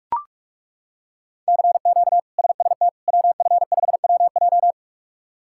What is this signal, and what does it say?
Morse code, heard as a keyed tone, about 41 wpm, 700 Hz. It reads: CQ SST WF5WJ
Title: K1USN SST QSO Practice - 45wpm